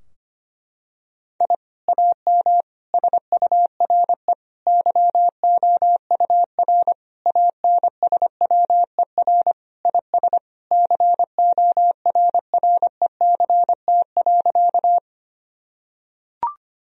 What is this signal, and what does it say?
Morse code, heard as a keyed tone, about 25 wpm, 700 Hz. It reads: I AM SURE YOUR ANSWER IS CORRECT.
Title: Sentences from Top 1000 Words - Review Part 2 - 25wpm